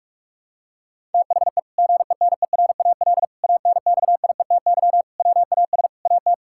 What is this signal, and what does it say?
Morse code, heard as a keyed tone, about 42 wpm, 700 Hz. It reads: THE GENERAL ANXIETY WAS AT
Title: Sentences from Top 1600 Words 45wpm